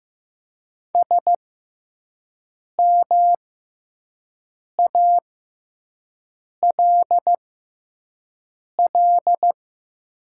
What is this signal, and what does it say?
Morse code, heard as a keyed tone, about 15 wpm, spaced as for 5 wpm, 700 Hz. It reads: SMALL
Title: 6x Character Spacing - Top 200 Words - 15wpm